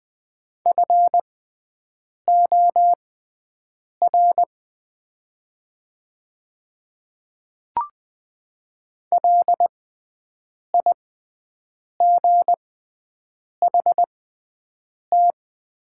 Morse code, heard as a keyed tone, about 20 wpm, spaced as for 7 wpm, 700 Hz. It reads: FOR LIGHT
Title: 6x Character Spacing - Top 300 Words - 20wpm